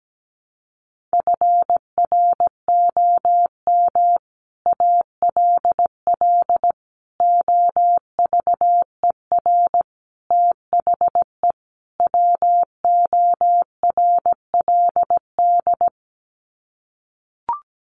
Morse code, heard as a keyed tone, about 17 wpm, 700 Hz. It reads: FROM ALL OVER THE WORLD
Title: Sets of 5 Words 17wpm